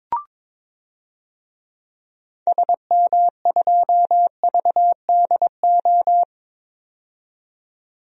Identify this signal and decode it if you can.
Morse code, heard as a keyed tone, about 21 wpm, 700 Hz. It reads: SM2VDO
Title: DXCC-100 Call Signs - 6 Characters - 22wpm